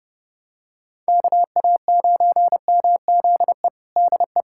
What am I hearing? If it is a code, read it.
Morse code, heard as a keyed tone, about 30 wpm, 700 Hz. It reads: KA9MZE DE
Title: USA Callsign DE Callsign 30wpm